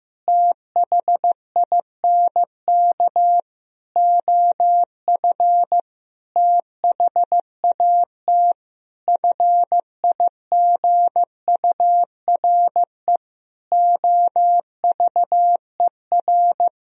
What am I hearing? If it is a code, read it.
Morse code, heard as a keyed tone, about 15 wpm, 700 Hz. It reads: THINK OF THAT FIGURE OVER
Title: Sentences from Top 1000 Words - Review Part 2 - 15wpm